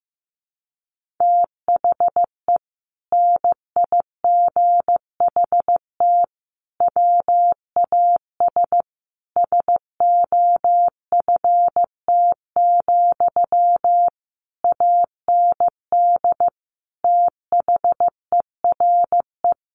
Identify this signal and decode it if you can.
Morse code, heard as a keyed tone, about 15 wpm, 700 Hz. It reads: THE NIGHT WAS SOFT, AND THERE
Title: Sentences from Top 1500 Words 15wpm